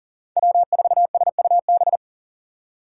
Morse code, heard as a keyed tone, about 40 wpm, 700 Hz. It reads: W4SUB
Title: USA Call Signs - 6x Character Spacing - 40wpm